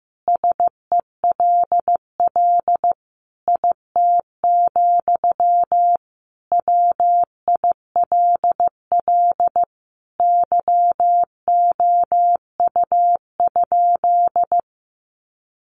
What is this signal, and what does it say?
Morse code, heard as a keyed tone, about 15 wpm, 700 Hz. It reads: SELL IT, WILL YOU?
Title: Sentences from Top 1100 Words 15wpm